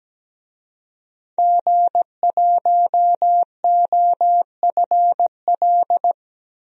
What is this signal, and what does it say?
Morse code, heard as a keyed tone, about 17 wpm, 700 Hz. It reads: G1OFL
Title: DXCC-100 Call Signs - 5 Characters - 17wpm